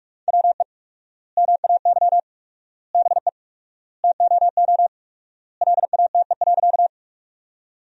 Morse code, heard as a keyed tone, about 46 wpm, 700 Hz. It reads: WE MAY BE TOO LATE.
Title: 4x Word-Spacing - Sentences from Top 300 Words - 45wpm